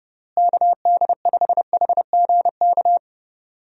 Morse code, heard as a keyed tone, about 30 wpm, 700 Hz. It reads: KD5HGK
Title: USA Call Signs - 6 Characters - Repeated 3x - 30wpm